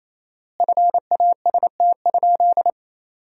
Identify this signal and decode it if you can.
Morse code, heard as a keyed tone, about 28 wpm, 700 Hz. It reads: FAST?
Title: 4x Word-Spacing - Sentences from Top 500 Words - 28wpm